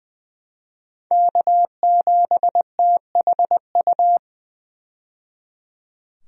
Morse code, heard as a keyed tone, about 20 wpm, 700 Hz. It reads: K7THU
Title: USA Call Signs - 5 Characters - 20wpm